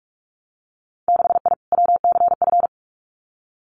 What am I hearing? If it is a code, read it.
Morse code, heard as a keyed tone, about 46 wpm, 700 Hz. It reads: 6I WCF
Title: Winter Field Day Exchanges Encore 45wpm